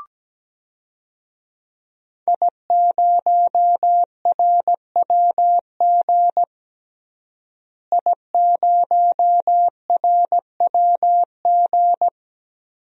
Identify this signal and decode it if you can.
Morse code, heard as a keyed tone, about 17 wpm, 700 Hz. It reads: I0RWG I0RWG
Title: DXCC-100 Call Signs - 5 Characters - Repeated 2x - 17wpm